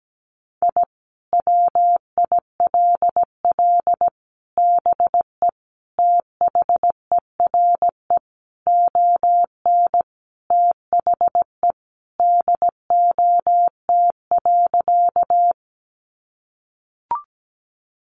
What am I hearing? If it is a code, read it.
Morse code, heard as a keyed tone, about 17 wpm, 700 Hz. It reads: I WILL BE THERE ON THE DOT.
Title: Sentences from Top 1500 Words 17wpm